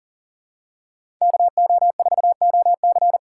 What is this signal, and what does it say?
Morse code, heard as a keyed tone, about 40 wpm, 700 Hz. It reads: KO4OC
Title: USA Call Signs - 5 Characters - 3x Character Spacing - 40wpm